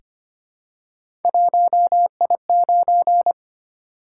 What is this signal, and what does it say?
Morse code, heard as a keyed tone, about 25 wpm, 700 Hz. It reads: 1I9
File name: Three Letter-Number Combinations - 25wpm